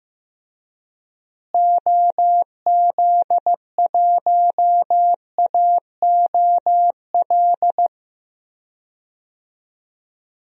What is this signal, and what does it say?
Morse code, heard as a keyed tone, about 15 wpm, 700 Hz. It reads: OZ1AOL
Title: DXCC-100 Call Signs - 6 Characters - Repeated 2x - 15wpm